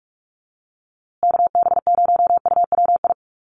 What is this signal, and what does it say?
Morse code, heard as a keyed tone, about 44 wpm, 700 Hz. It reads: KB0UWI
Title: USA Call Signs Repeated 2x 45wpm